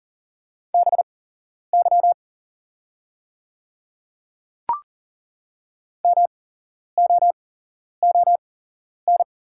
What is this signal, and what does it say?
Morse code, heard as a keyed tone, about 41 wpm, spaced as for 11 wpm, 700 Hz. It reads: BY MOON
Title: 8x Character Spacing - Top 500 Words - 40wpm